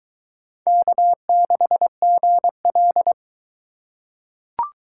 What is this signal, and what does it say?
Morse code, heard as a keyed tone, about 21 wpm, 700 Hz. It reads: K6GL